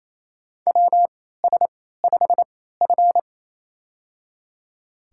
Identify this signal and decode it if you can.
Morse code, heard as a keyed tone, about 27 wpm, spaced as for 16 wpm, 700 Hz. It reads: WS5F
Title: USA Call Signs - 4 Characters - 3x Character Spacing - 28wpm